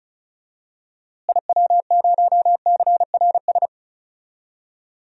Morse code, heard as a keyed tone, about 36 wpm, 700 Hz. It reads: IW0CRS
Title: DXCC-100 Call Signs - 6 Characters - Repeated 3x - 35wpm